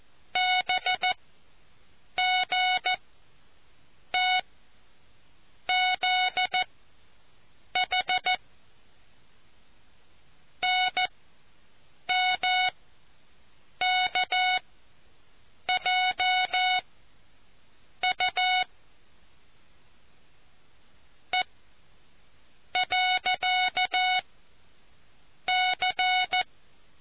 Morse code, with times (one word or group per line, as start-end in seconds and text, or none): 0.4-4.4 BGT
5.7-8.3 ZH
10.6-16.8 NMKJ
18.0-18.6 U
21.3-21.4 E
22.8-24.2 .
25.5-26.4 C